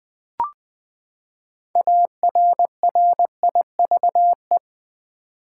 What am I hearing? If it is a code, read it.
Morse code, heard as a keyed tone, about 20 wpm, 700 Hz. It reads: ARRIVE
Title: Six Letter Words - 20wpm